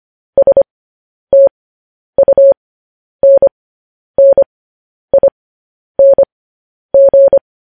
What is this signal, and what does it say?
Morse code, heard as a keyed tone, about 25 wpm, spaced as for 10 wpm, 550 Hz. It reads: STUNNING
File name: podcast / News headlines-1x-25x10